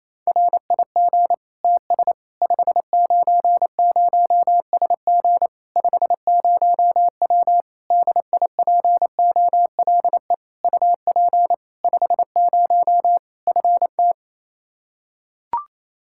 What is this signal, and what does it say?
Morse code, heard as a keyed tone, about 28 wpm, 700 Hz. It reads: RIG TS 590SG 50W DIPOLE UP 50 FT
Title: Practice QSO - Exchange 2 - Info - 28wpm